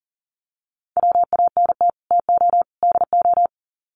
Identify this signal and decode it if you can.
Morse code, heard as a keyed tone, about 40 wpm, 700 Hz. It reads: WANT TO DO
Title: Sets of 3 Words 40wpm